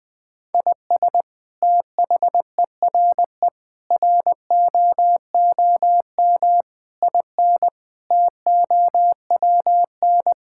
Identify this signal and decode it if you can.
Morse code, heard as a keyed tone, about 20 wpm, 700 Hz. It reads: IS THERE ROOM IN TOWN
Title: Sentences from Top 1000 Words - Review Part 5 - 20wpm